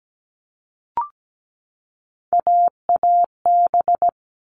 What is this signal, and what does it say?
Morse code, heard as a keyed tone, about 17 wpm, 700 Hz. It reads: AAB
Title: All Cut Numbers - 3-Digit 17wpm